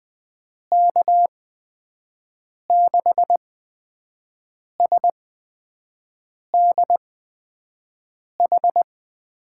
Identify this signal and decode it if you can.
Morse code, heard as a keyed tone, about 20 wpm, spaced as for 5 wpm, 700 Hz. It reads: K6SDH
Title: USA Call Signs - 6 Characters - 8x Character Spacing - 20wpm